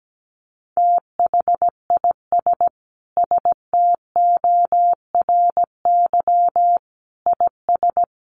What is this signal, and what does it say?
Morse code, heard as a keyed tone, about 17 wpm, 700 Hz. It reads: THIS STORY IS
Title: Sentences from Top 1300 Words 17wpm